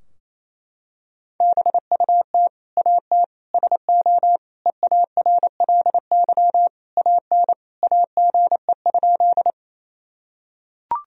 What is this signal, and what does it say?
Morse code, heard as a keyed tone, about 27 wpm, 700 Hz. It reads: BUT AT SO EARLY AN AGE?
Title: Sentences from Top 1000 Words - Review Part 3 - 28wpm